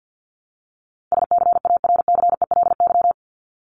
Morse code, heard as a keyed tone, about 50 wpm, 700 Hz. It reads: SCARCELY
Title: LY Suffix - Top 100 Words with Base Words 50wpm